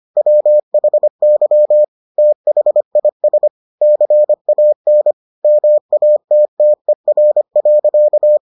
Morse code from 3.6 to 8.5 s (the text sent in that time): CAN MATTER.